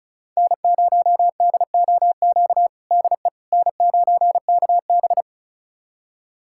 Morse code, heard as a keyed tone, about 34 wpm, 700 Hz. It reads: N0DOQ DE N9KB
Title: USA Callsign DE Callsign 35wpm